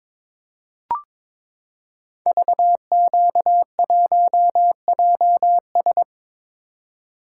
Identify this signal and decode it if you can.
Morse code, heard as a keyed tone, about 22 wpm, 700 Hz. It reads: VQ1JS
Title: Code Groups - 5 Letter:Numbers - 22wpm